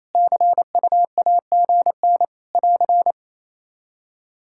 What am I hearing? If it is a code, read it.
Morse code, heard as a keyed tone, about 27 wpm, 700 Hz. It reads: CUAGN <AR>
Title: Practice QSO - Exchange 4 - Ending - 28wpm